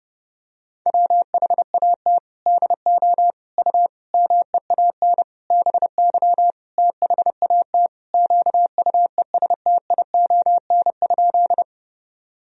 Morse code, heard as a keyed tone, about 32 wpm, 700 Hz. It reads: WHAT DO U MEAN BY THAT QUESTION?
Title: Sentences from Top 400 Words and Abbreviations 30wpm